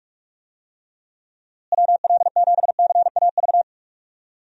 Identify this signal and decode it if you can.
Morse code, heard as a keyed tone, about 46 wpm, 700 Hz. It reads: WL7CAV
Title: USA Call Signs - 6 Characters - Repeated 3x - 45wpm